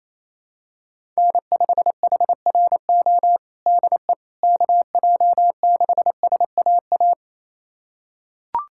Morse code, heard as a keyed tone, about 28 wpm, 700 Hz. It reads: N5HRO DE KJ6SAA
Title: USA Callsign DE Callsign 28wpm